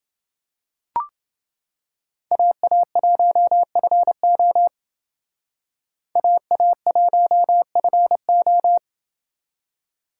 Morse code, heard as a keyed tone, about 30 wpm, 700 Hz. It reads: AA1FO AA1FO